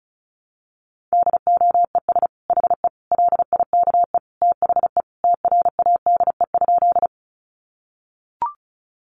Morse code, heard as a keyed tone, about 34 wpm, 700 Hz. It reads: DOES HE LIKE THE TRADE?
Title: Sentences from Top 600 Words 35wpm